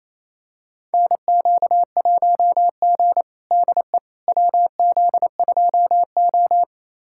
Morse code, heard as a keyed tone, about 28 wpm, 700 Hz. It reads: NQ1G DE WZ2O
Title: USA Callsign DE Callsign - 4 Characters - 28wpm